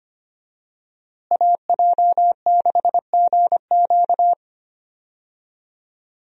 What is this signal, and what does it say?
Morse code, heard as a keyed tone, about 25 wpm, 700 Hz. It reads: AJ6GQ
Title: USA Call Signs - 5 Characters - Repeated 3x - 25wpm